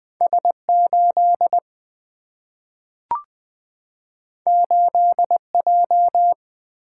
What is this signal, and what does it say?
Morse code, heard as a keyed tone, about 20 wpm, 700 Hz. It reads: S8 8J
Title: Single-Letter Single-Number 20wpm